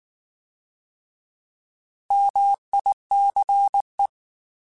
Morse code, heard as a keyed tone, about 19 wpm, 800 Hz. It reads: MICE
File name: Alphabetical / Alphabetical mp3s / Ms @18